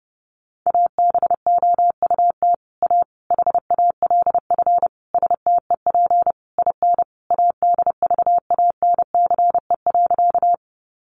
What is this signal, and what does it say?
Morse code, heard as a keyed tone, about 30 wpm, 700 Hz. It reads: ABOUT A HALF STEP IN ADVANCE.
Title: Sentences from Top 1400 Words 30wpm